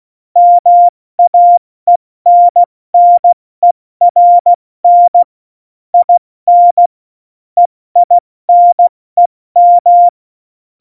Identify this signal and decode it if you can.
Morse code, heard as a keyed tone, about 16 wpm, 700 Hz. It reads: MAENNERN IN EINEM
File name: Morsekurs / Witze / Witz 19. Januar 2021, 70 F80